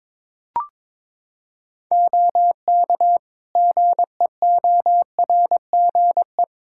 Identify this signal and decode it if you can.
Morse code, heard as a keyed tone, about 22 wpm, 700 Hz. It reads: OK GEORGE
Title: K1USN SST QSO Practice 22wpm